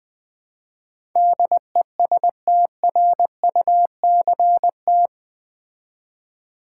Morse code, heard as a keyed tone, about 20 wpm, 700 Hz. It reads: DESTRUCT
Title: Build A Word - 4 Letters At A Time - 11-Letter Words - 20wpm